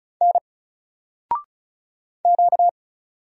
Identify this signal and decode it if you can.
Morse code, heard as a keyed tone, about 34 wpm, 700 Hz. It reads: N Q
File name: Single-Letter-Number-35wpm